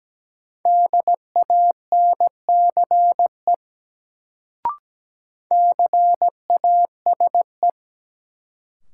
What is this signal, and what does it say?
Morse code, heard as a keyed tone, about 17 wpm, 700 Hz. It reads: DANCE CASE
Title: Sentences from Top 700 Words - New Word Drill - 17wpm